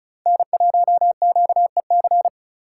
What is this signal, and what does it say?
Morse code, heard as a keyed tone, about 44 wpm, 700 Hz. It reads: N1QEC